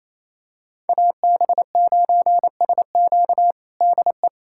Morse code, heard as a keyed tone, about 28 wpm, 700 Hz. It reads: AB9SQ DE
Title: USA Callsign DE Callsign 28wpm